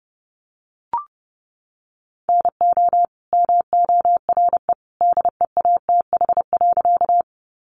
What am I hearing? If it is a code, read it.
Morse code, heard as a keyed tone, about 30 wpm, 700 Hz. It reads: NO MORE DEATH.
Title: Sentences from Top 1100 Words 30wpm